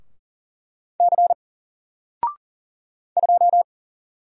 Morse code, heard as a keyed tone, about 40 wpm, 700 Hz. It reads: C 2
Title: Single Letter-Number Alt 40wpm